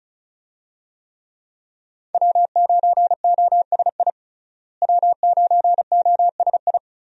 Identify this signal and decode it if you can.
Morse code, heard as a keyed tone, about 36 wpm, 700 Hz. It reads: W9OSI W9OSI
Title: USA Call Signs - 5 Characters - Repeated 3x - 35wpm